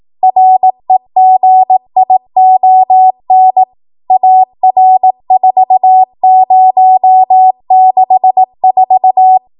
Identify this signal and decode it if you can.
Morse code, heard as a keyed tone, about 18 wpm, 750 Hz. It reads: REGION AR4064
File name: Morse / 18WPM